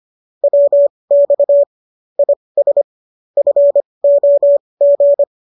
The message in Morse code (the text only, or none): WX IS FOG